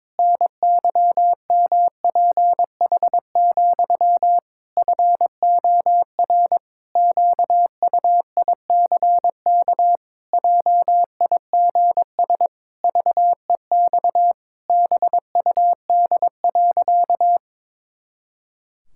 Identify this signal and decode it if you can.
Morse code, heard as a keyed tone, about 22 wpm, 700 Hz. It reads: NYMPH, FOR QUICK JIGS VEX BUD.
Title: Pangram Challenge v2 - 22wpm